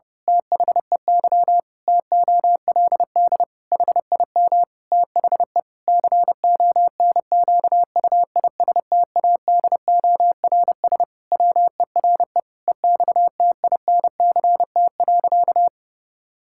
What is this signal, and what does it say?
Morse code, heard as a keyed tone, about 30 wpm, 700 Hz. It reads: THEY TOLD HIM THE CONQUISTADORS WERE EXTINCT.
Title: Pangram Challenge v2 - 30wpm